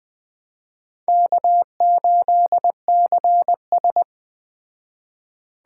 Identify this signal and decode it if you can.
Morse code, heard as a keyed tone, about 20 wpm, 700 Hz. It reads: K8CS